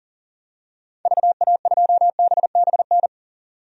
Morse code, heard as a keyed tone, about 40 wpm, 700 Hz. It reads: VA2BBN
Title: DXCC-100 Call Signs - 6 Characters - 40wpm